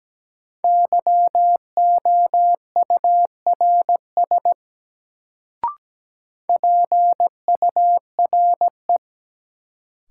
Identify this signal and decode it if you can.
Morse code, heard as a keyed tone, about 17 wpm, 700 Hz. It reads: YOURS PURE
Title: Sentences from Top 1500 Words - New Word Drill - 17wpm